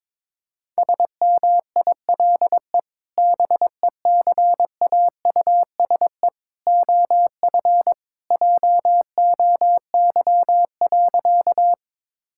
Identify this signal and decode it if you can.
Morse code, heard as a keyed tone, about 22 wpm, 700 Hz. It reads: SMILE BECAUSE OF JOY.
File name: practice / 22wpm / Sentences with 2-Word Phrases - 22wpm